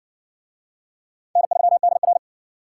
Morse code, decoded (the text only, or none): N3DR